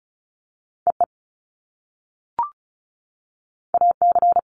Morse code, heard as a keyed tone, about 36 wpm, 700 Hz. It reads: EE AC
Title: Top Two Letter Combinations in Words 35wpm